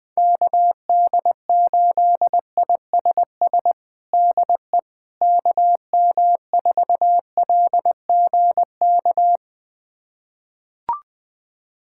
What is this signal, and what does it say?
Morse code, heard as a keyed tone, about 20 wpm, 700 Hz. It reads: KD8ISS DE KM4LGK
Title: USA Callsign DE Callsign - 6 Characters - 20wpm